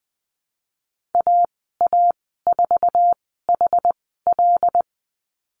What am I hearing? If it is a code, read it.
Morse code, heard as a keyed tone, about 20 wpm, spaced as for 14 wpm, 700 Hz. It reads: AA4HL
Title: 076 Lesson 13g - Letter L - Call Signs 20wpm